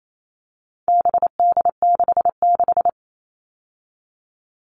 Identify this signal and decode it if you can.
Morse code, heard as a keyed tone, about 28 wpm, 700 Hz. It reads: BD66
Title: All Cut Numbers - 4-Digit 28wpm